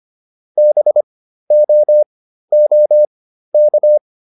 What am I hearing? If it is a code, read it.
Morse code, heard as a keyed tone, about 25 wpm, spaced as for 13 wpm, 600 Hz. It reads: BOOK